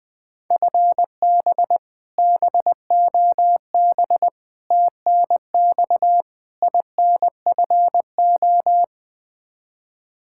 Morse code, heard as a keyed tone, about 21 wpm, 700 Hz. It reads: FB BOB TNX INFO
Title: Practice QSO - Exchange 3 - More Info - 20wpm